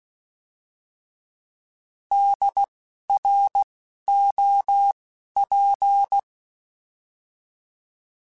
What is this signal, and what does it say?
Morse code, heard as a keyed tone, about 16 wpm, spaced as for 12 wpm, 800 Hz. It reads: DROP
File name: Alphabetical / Alphabetical mp3s / Ds @10